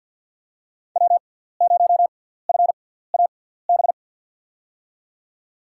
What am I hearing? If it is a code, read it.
Morse code, heard as a keyed tone, about 52 wpm, spaced as for 17 wpm, 700 Hz. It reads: W0FAB
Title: USA Call Signs - 6x Character Spacing - 50wpm